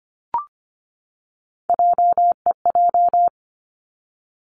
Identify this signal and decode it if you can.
Morse code, heard as a keyed tone, about 25 wpm, 700 Hz. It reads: JEJ
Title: Three Challenging Letter-Number Combinations 25wpm